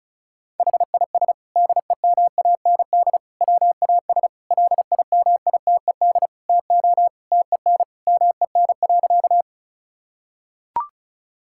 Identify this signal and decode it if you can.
Morse code, heard as a keyed tone, about 36 wpm, 700 Hz. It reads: HIS DEMAND WAS LIMITED TO TEN MEN.
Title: Sentences from Top 1000 Words - Review Part 6 - 35wpm